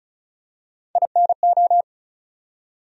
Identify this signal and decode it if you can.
Morse code, heard as a keyed tone, about 36 wpm, 700 Hz. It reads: INO
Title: USA Call Sign Postfixes - 3 Letters 35wpm